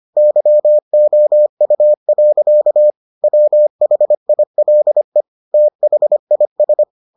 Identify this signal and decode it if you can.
Morse code, heard as a keyed tone, about 25 wpm, 600 Hz. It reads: YOU. WHILE THIS